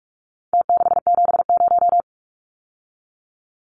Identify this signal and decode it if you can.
Morse code, heard as a keyed tone, about 44 wpm, 700 Hz. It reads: T670
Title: Numbers - Contest Style - 4-Digit 45wpm